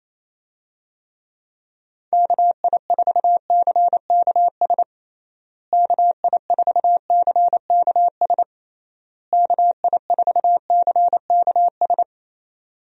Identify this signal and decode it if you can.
Morse code, heard as a keyed tone, about 28 wpm, 700 Hz. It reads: KI4CKS KI4CKS KI4CKS
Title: USA Call Signs - 6 Characters - Repeated 3x - 28wpm